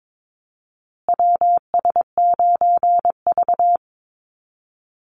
Morse code, heard as a keyed tone, about 22 wpm, 700 Hz. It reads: WS9V
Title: Call Signs Repeated 2x - 22wpm